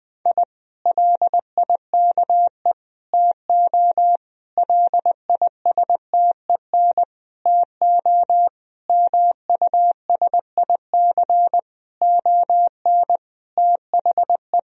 Morse code, heard as a keyed tone, about 21 wpm, 700 Hz. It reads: I LIKE TO LISTEN TO MUSIC ON THE
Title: Sentences from Top 1000 Words - Review Part 3 - 20wpm